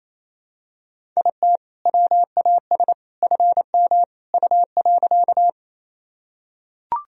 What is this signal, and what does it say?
Morse code, heard as a keyed tone, about 28 wpm, 700 Hz. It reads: IT WAS FM U.
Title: Sentences from Top 100 Words and Abbreviations 28wpm